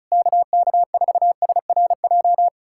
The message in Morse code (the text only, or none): KK4SRJ